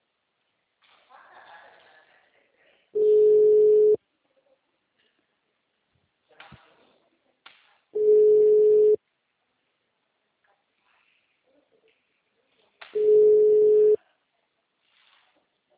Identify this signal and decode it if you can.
Ringback tone (Europe, 425 Hz), with speech.